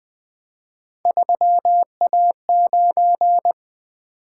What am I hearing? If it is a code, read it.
Morse code, heard as a keyed tone, about 20 wpm, 700 Hz. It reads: 3A9
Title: Three Letter-Number Combinations - 20wpm